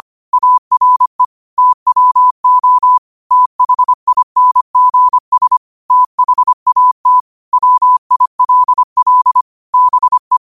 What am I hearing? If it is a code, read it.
Morse code, heard as a keyed tone, about 25 wpm, 1000 Hz. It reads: ARE TWO THINGS THAT WILL BE